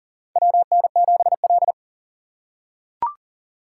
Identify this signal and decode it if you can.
Morse code, heard as a keyed tone, about 37 wpm, 700 Hz. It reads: WN7L
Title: USA Call Signs - 6x Character Spacing - 40wpm